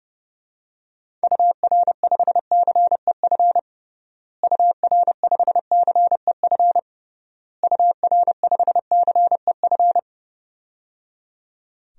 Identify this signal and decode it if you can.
Morse code, heard as a keyed tone, about 30 wpm, 700 Hz. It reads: UR5CEF UR5CEF UR5CEF
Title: DXCC-100 Call Signs - 6 Characters - Repeated 3x - 30wpm